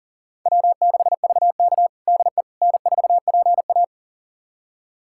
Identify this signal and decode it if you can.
Morse code, heard as a keyed tone, about 40 wpm, 700 Hz. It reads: W6VK DE N4PA
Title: USA Callsign DE Callsign - 4 Characters - 40wpm